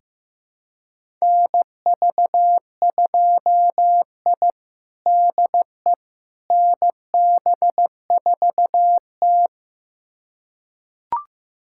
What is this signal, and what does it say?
Morse code, heard as a keyed tone, about 15 wpm, 700 Hz. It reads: NV2I DE NB4T
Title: USA Callsign DE Callsign - 4 Characters - 15wpm